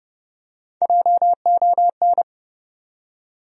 Morse code, heard as a keyed tone, about 30 wpm, 700 Hz. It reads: JON
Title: USA Common Names - Max 4 Letters - 30wpm